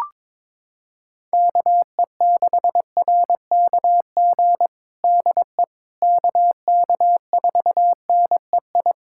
Morse code, heard as a keyed tone, about 22 wpm, 700 Hz. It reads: KE6RKG DE KK4NEI
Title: USA Callsign DE Callsign 22wpm